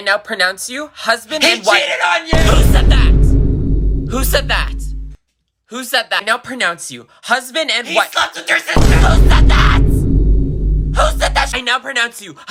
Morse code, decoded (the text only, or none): none